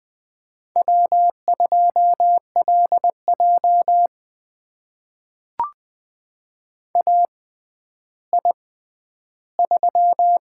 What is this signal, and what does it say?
Morse code, heard as a keyed tone, about 20 wpm, spaced as for 11 wpm, 700 Hz. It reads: W2LJ AI3
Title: USA Call Signs - 4 Characters - 6x Character Spacing - 20wpm